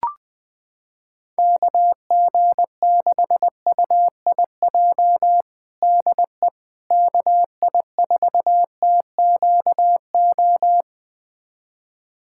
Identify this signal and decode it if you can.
Morse code, heard as a keyed tone, about 20 wpm, 700 Hz. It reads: KG6UIJ DE KI4TQO
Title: USA Callsign DE Callsign - 6 Characters - 20wpm